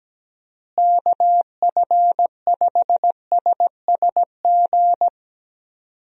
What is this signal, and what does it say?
Morse code, heard as a keyed tone, about 17 wpm, 700 Hz. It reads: KF5SSG
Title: USA Call Signs - 6 Characters - Repeated 3x - 17wpm